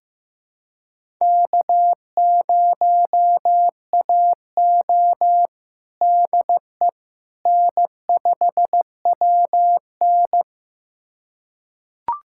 Morse code, heard as a keyed tone, about 15 wpm, 700 Hz. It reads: K0AO DE N5WN
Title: USA Callsign DE Callsign - 4 Characters - 15wpm